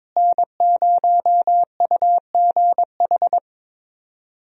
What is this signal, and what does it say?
Morse code, heard as a keyed tone, about 21 wpm, 700 Hz. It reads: N0UGH
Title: USA Call Signs - 5 Characters - Repeated 3x - 22wpm